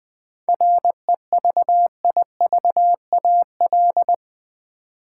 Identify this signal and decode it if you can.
Morse code, heard as a keyed tone, about 20 wpm, 700 Hz. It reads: REVIVAL
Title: 7 Letter Words - First 3 Letters Given - 20wpm